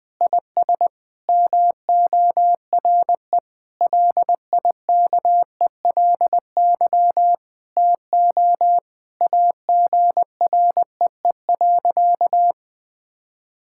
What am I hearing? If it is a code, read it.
Morse code, heard as a keyed tone, about 20 wpm, 700 Hz. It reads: IS MORE LIKELY TO AGREE.
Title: Sentences with 3-Word Phrases - Spoken Beginning - 20wpm